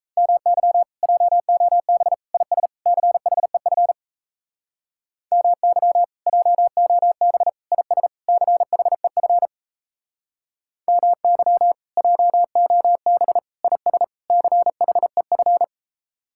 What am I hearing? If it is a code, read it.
Morse code, heard as a keyed tone, about 37 wpm, 700 Hz. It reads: MY JOB IS CHEF MY JOB IS CHEF MY JOB IS CHEF